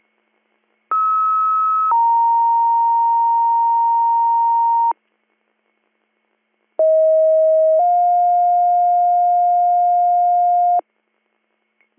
Two-tone page, heard as a one-second tone, then a three-second tone, about 1290 then 930 Hz, about 640 then 710 Hz. It is heard from 0.9 to 10.8 s.